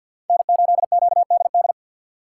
Morse code, heard as a keyed tone, about 49 wpm, 700 Hz. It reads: N8QDD
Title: USA Call Signs - 3x Character Spacing - 50wpm